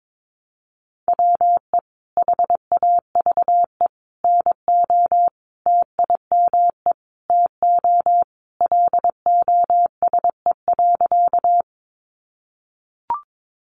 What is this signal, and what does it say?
Morse code, heard as a keyed tone, about 22 wpm, 700 Hz. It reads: WE HAVE NO TIME TO LOSE.
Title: Sentences from Top 1100 Words 22wpm